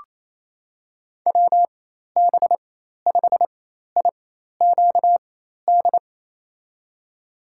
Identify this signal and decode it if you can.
Morse code, heard as a keyed tone, about 28 wpm, spaced as for 13 wpm, 700 Hz. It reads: WB5IQD
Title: USA Call Signs - 6 Characters - 4x Character Spacing - 28wpm